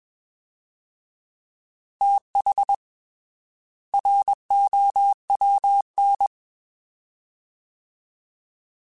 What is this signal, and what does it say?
Morse code, heard as a keyed tone, about 21 wpm, 800 Hz. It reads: TH ROWN
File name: TH / TH mp3s / Fast TH @20